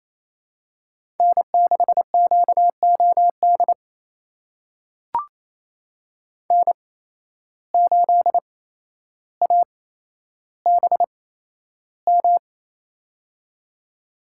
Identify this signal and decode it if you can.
Morse code, heard as a keyed tone, about 28 wpm, spaced as for 12 wpm, 700 Hz. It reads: N6QOD N8ABM
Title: USA Call Signs - 5 Characters - 8x Character Spacing - 28wpm